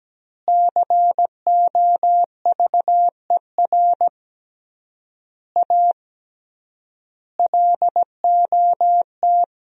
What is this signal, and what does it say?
Morse code, heard as a keyed tone, about 17 wpm, 700 Hz. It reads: COVER A LOT
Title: 3x Word-Spacing - Sentences from Top 500 Words - 17wpm